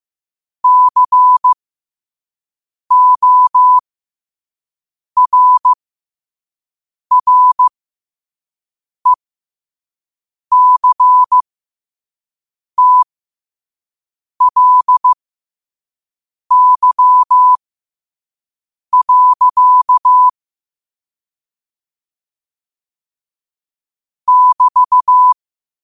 Morse code, heard as a keyed tone, about 15 wpm, spaced as for 5 wpm, 1000 Hz. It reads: CORRECTLY. <BT>
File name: qotdfiles / 5wpm